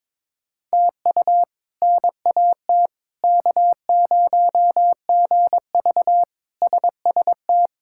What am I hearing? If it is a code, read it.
Morse code, heard as a keyed tone, about 22 wpm, 700 Hz. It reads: TU NAT K0GV SST
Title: K1USN SST QSO Practice - 22wpm